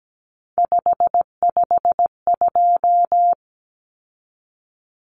Morse code, heard as a keyed tone, about 17 wpm, 700 Hz. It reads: 552
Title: Numbers - 3-Digit 17wpm